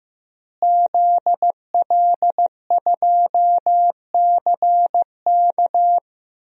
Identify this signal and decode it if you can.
Morse code, heard as a keyed tone, about 15 wpm, 700 Hz. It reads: ZL2CK
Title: DXCC-100 Call Signs - 5 Characters - Repeated 3x - 15wpm